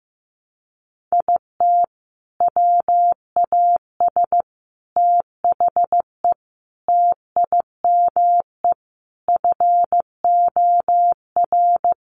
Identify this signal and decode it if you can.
Morse code, heard as a keyed tone, about 15 wpm, 700 Hz. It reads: IT WAS THE TIME FOR
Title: Sentences from Top 700 Words 15wpm